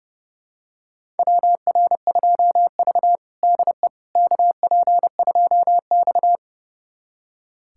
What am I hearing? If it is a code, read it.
Morse code, heard as a keyed tone, about 30 wpm, 700 Hz. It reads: WR2V DE KP2X
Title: USA Callsign DE Callsign - 4 Characters - 30wpm